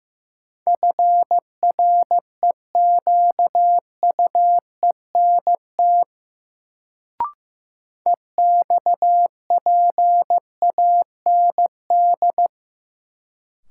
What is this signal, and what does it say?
Morse code, heard as a keyed tone, about 15 wpm, 700 Hz. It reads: FREQUENT EXPAND
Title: Sentences from Top 1400 Words - New Word Drill Alt - 15wpm